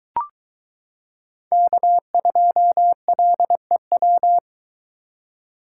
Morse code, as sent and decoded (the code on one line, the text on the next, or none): -.- ..--- .-.. .--
K2LEW